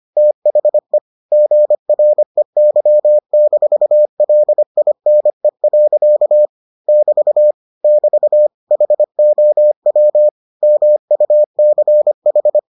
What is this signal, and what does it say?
Morse code, heard as a keyed tone, about 25 wpm, 600 Hz. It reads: THE GREY-LINE. <BT> <BT> HOW MUCH